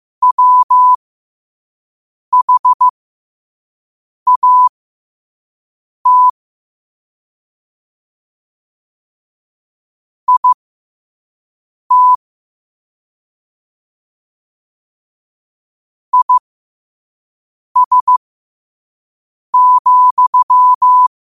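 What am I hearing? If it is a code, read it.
Morse code, heard as a keyed tone, about 15 wpm, spaced as for 5 wpm, 1000 Hz. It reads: WHAT IT IS,
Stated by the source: Morse code at 5 wpm.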